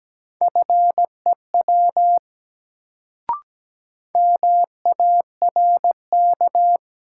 Morse code, heard as a keyed tone, about 17 wpm, 700 Hz. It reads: FEW MARK
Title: Sentences from Top 300 Words - New Word Drill - 17wpm